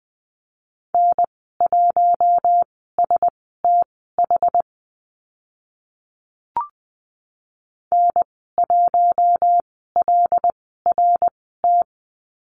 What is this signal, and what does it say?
Morse code, heard as a keyed tone, about 20 wpm, spaced as for 14 wpm, 700 Hz. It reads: N1STH N1LRT
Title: 076 Lesson 13g - Letter L - Call Signs 20wpm